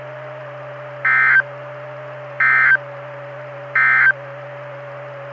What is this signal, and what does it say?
Emergency Alert System (SAME) tones, heard as harsh buzzing bursts. It is heard from 0.9 to 4.2 s.
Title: Saline, AK SKYWARN Repeater System - W5RHS